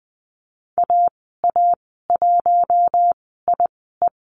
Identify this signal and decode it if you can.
Morse code, heard as a keyed tone, about 20 wpm, spaced as for 14 wpm, 700 Hz. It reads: AA1IE
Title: 082 Lesson 14g - Number 2 - Call Signs 20wpm